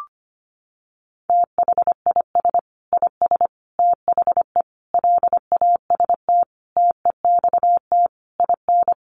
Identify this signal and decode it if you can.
Morse code, heard as a keyed tone, about 25 wpm, 700 Hz. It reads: THIS IS THE LAST TEXT IN
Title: Sentences from Top 1200 Words 25wpm